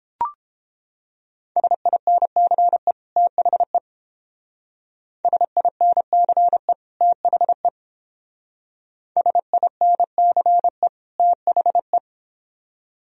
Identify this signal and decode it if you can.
Morse code, heard as a keyed tone, about 30 wpm, 700 Hz. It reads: SINCE THE SINCE THE SINCE THE